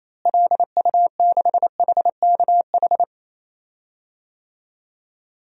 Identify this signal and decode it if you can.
Morse code, heard as a keyed tone, about 28 wpm, 700 Hz. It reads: LU6HKH
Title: DXCC-100 Call Signs - 6 Characters - Repeated 3x - 28wpm